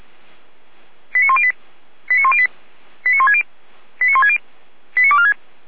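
Five-tone selective call (ZVEI), heard as quick runs of five tones, tones of about 70 ms. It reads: 89108 89108 89160 89160 89246
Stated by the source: Five-tone ZVEI selective call 89072, received by SDR.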